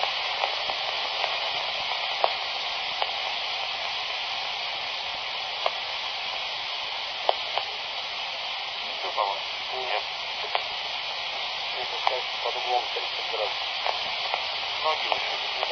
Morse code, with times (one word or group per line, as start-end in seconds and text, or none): none